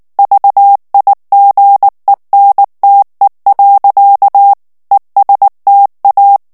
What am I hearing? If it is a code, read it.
Morse code, heard as a keyed tone, about 19 wpm, 800 Hz. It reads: VIGENTE. ESTA
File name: morse mp3 / ESPAGNOL TEXTE EN CLAIR 2 VITESSE 18 MOTS MINUTES